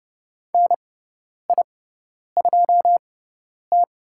Morse code, heard as a keyed tone, about 30 wpm, spaced as for 10 wpm, 700 Hz. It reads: NI2T
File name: Lesson 17g - Letter u - Call Signs 30wpm